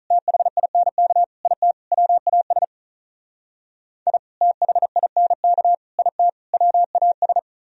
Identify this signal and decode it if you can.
Morse code, heard as a keyed tone, about 45 wpm, 700 Hz. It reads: THINK IT WAS I THINK IT WAS